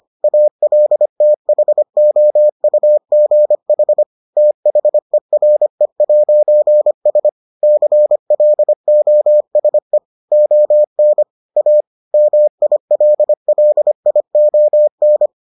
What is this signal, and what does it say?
Morse code, heard as a keyed tone, about 25 wpm, 600 Hz. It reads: ALTHOUGH THERE'S CLOSE ON A MILLION